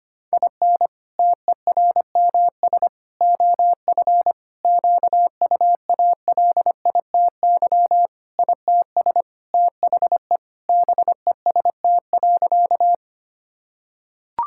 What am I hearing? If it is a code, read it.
Morse code, heard as a keyed tone, about 25 wpm, 700 Hz. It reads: IN TERMS OF QUALITY ITS THE BEST.
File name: practice / 25wpm / Sentences with 3-Word Phrases - 25wpm